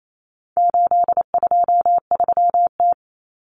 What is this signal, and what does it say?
Morse code, heard as a keyed tone, about 28 wpm, 700 Hz. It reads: 823T
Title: Common Cut Numbers - 4-Digit 28wpm